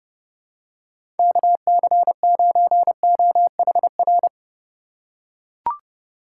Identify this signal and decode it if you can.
Morse code, heard as a keyed tone, about 30 wpm, 700 Hz. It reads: KC9OHR
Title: USA Call Signs - 3x Character Spacing - 30wpm